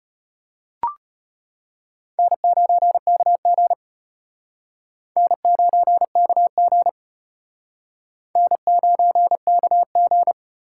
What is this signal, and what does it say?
Morse code, heard as a keyed tone, about 33 wpm, 700 Hz. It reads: N9KG N9KG N9KG